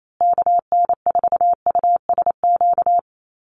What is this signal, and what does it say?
Morse code, heard as a keyed tone, about 28 wpm, 700 Hz. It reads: KN4USQ
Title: USA Call Signs Repeated 3x 28wpm